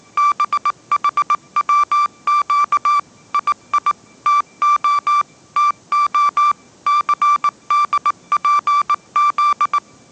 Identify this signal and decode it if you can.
Morse code, heard as a keyed tone, about 19 wpm, 1170 Hz. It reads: BHWQ II TO TO CDPZ
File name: 20wpm